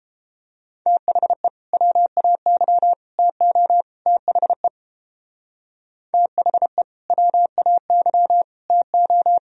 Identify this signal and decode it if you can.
Morse code, heard as a keyed tone, about 31 wpm, 700 Hz. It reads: THE WAY TO THE THE WAY TO